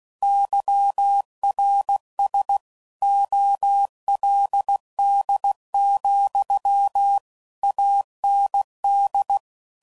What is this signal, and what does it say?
Morse code, heard as a keyed tone, about 16 wpm, 800 Hz. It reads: YRS OLD, AND
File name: CW Code Groups @15WPM / QSOs / kc7sac